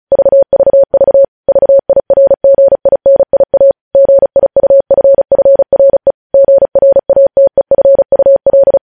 Morse code, heard as a keyed tone, about 36 wpm, 550 Hz. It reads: VVV VIRGINIA GIUFFRE GRATEFUL
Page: news headlines in morse code at 35 wpm